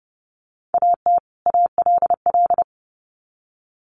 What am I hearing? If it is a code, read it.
Morse code, heard as a keyed tone, about 30 wpm, 700 Hz. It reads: AT ALL
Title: Sets of 2 Words 30wpm